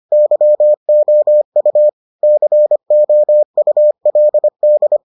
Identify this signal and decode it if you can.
Morse code, heard as a keyed tone, about 25 wpm, 600 Hz. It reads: YOU COULD